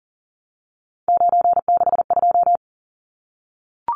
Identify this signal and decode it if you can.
Morse code, heard as a keyed tone, about 40 wpm, 700 Hz. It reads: 962
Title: Numbers - 3-Digit 40wpm